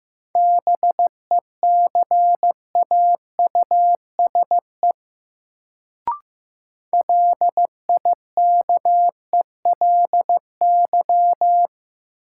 Morse code, heard as a keyed tone, about 15 wpm, 700 Hz. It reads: BECAUSE LIKELY
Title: Sentences from Top 600 Words - New Word Drill - 15wpm